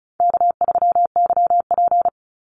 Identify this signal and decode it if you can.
Morse code, heard as a keyed tone, about 36 wpm, 700 Hz. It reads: K3YP
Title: Call Signs - 35wpm